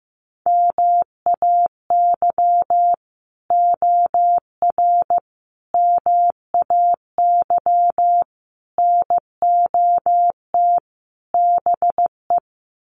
Morse code, heard as a keyed tone, about 15 wpm, 700 Hz. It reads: MAY OR MAY NOT BE
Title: Sets of 5 Words 15wpm